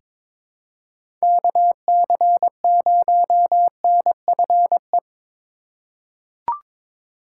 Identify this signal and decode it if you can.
Morse code, heard as a keyed tone, about 22 wpm, 700 Hz. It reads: KC0NFE
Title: USA Call Signs - 3x Character Spacing - 22wpm